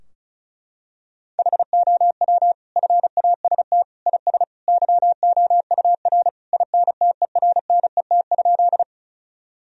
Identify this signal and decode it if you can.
Morse code, heard as a keyed tone, about 36 wpm, 700 Hz. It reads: HOW FAST IS YOUR INTERNET?
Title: Sentences from Top 1000 Words - Review Part 4 - 35wpm